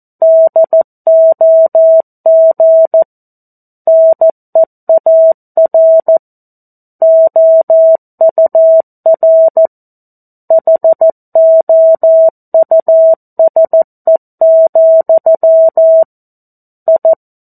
Morse code, heard as a keyed tone, about 14 wpm, 650 Hz. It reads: DOG NEAR OUR HOUSE, I
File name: 02-Chapter-02 14 WPM